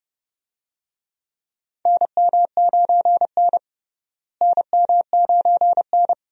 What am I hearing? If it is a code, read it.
Morse code, heard as a keyed tone, about 30 wpm, 700 Hz. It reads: NM9N NM9N
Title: USA Call Signs - 4 Characters - Repeated 2x - 30wpm